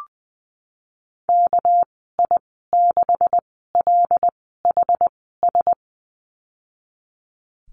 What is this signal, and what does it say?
Morse code, heard as a keyed tone, about 19 wpm, spaced as for 14 wpm, 700 Hz. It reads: KI6LHS
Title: 207 Lesson 33g - Letter j - Call Signs 20wpm